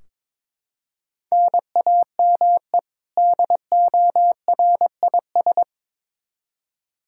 Morse code, heard as a keyed tone, about 22 wpm, 700 Hz. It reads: NAME DORIS
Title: Practice QSO - Exchange 1 - Beginning - 22wpm